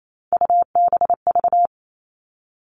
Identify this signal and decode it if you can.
Morse code, heard as a keyed tone, about 28 wpm, 700 Hz. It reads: UBV
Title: All Cut Numbers - 3-Digit 28wpm